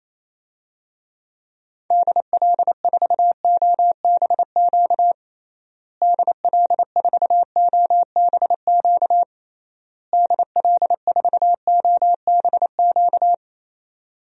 Morse code, heard as a keyed tone, about 28 wpm, 700 Hz. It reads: DL4OBQ DL4OBQ DL4OBQ
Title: USA Call Signs - 6 Characters - Repeated 3x - 28wpm